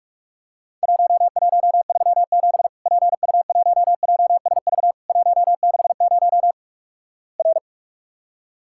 Morse code, heard as a keyed tone, about 48 wpm, 700 Hz. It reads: 1137 PU1JSV 160 R
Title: Worked All Europe DX Contest - QTC Exchange - 45wpm